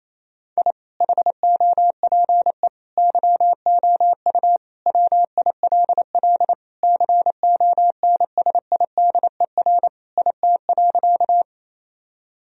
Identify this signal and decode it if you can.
Morse code, heard as a keyed tone, about 28 wpm, 700 Hz. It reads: I HOPE YOU WILL CONSIDER IT.
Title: Sentences from Top 1000 Words - Review Part 4 - 28wpm